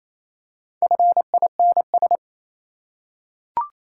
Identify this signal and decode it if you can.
Morse code, heard as a keyed tone, about 27 wpm, 690 Hz. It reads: FINS
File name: Easy Words - Speed-Racing with Hints - 22wpm